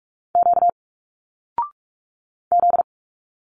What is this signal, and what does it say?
Morse code, heard as a keyed tone, about 45 wpm, 700 Hz. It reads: Q Z
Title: Single Letters 45wpm